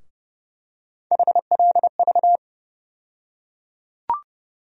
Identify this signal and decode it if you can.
Morse code, heard as a keyed tone, about 30 wpm, 700 Hz. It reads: HLV
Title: USA Call Sign Postfixes - 3 Letters 30wpm